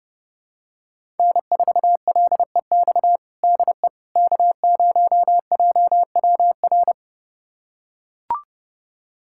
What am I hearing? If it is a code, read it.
Morse code, heard as a keyed tone, about 30 wpm, 700 Hz. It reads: N4LEX DE K0JWR
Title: USA Callsign DE Callsign - 5 Characters - 30wpm